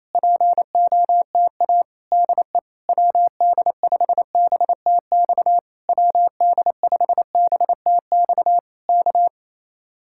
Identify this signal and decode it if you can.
Morse code, heard as a keyed tone, about 27 wpm, 700 Hz. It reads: POTA DE WD5BTX WD5BTX K
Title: Speed-Racing Common QSO Phrases Encore - 28wpm